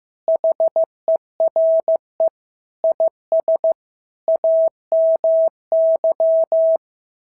Morse code, heard as a keyed tone, about 14 wpm, 650 Hz. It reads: HERE IS AMY
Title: SOTA QSO Practice 15wpm